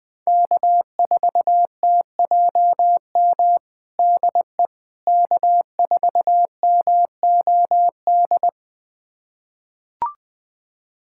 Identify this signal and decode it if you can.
Morse code, heard as a keyed tone, about 20 wpm, 700 Hz. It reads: K4TJM DE K4MOD
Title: USA Callsign DE Callsign - 5 Characters - 20wpm